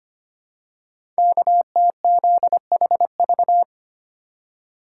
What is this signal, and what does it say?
Morse code, heard as a keyed tone, about 25 wpm, 700 Hz. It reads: KTZHV
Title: Code Groups - 5 Letters - 25wpm